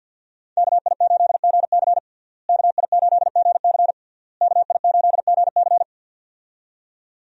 Morse code, heard as a keyed tone, about 50 wpm, 700 Hz. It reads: KI8GC KI8GC KI8GC
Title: USA Call Signs - 5 Characters - Repeated 3x - 50wpm